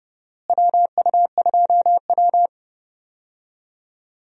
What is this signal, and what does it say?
Morse code, heard as a keyed tone, about 31 wpm, 700 Hz. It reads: WU2W
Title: USA Call Signs - 4 Characters - Repeated 3x - 30wpm